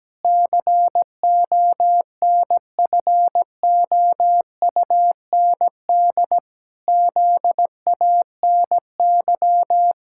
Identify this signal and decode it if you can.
Morse code, heard as a keyed tone, about 17 wpm, 700 Hz. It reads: CONFOUND ZANY
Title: Pangram Challenge v2 - 17wpm